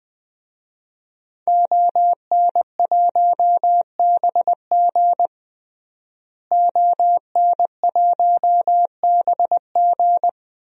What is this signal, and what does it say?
Morse code, heard as a keyed tone, about 21 wpm, 700 Hz. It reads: ON1BG ON1BG
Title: DXCC-100 Call Signs - 5 Characters - Repeated 3x - 20wpm